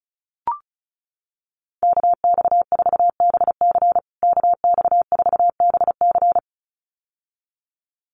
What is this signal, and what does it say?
Morse code, heard as a keyed tone, about 34 wpm, 700 Hz. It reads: KX4BC KX4BC
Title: USA Call Signs Repeated 2x 35wpm